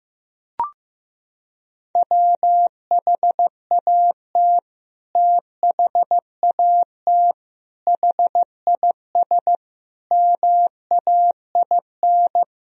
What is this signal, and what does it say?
Morse code, heard as a keyed tone, about 16 wpm, 700 Hz. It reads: WHAT THAT HIS MAIN
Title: Sentences from Top 1000 Words - Review Part 2 - 15wpm